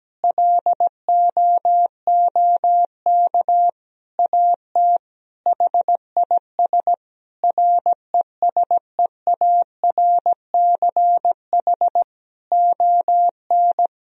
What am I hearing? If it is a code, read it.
Morse code, heard as a keyed tone, about 17 wpm, 700 Hz. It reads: LOOK AT HIS RESEARCH ON
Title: Sentences from Top 1000 Words - Review Part 4 - 17wpm